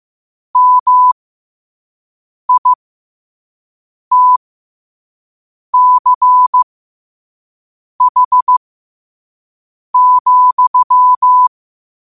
Morse code, heard as a keyed tone, about 15 wpm, spaced as for 5 wpm, 1000 Hz. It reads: MITCH,
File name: qotdfiles / 5wpm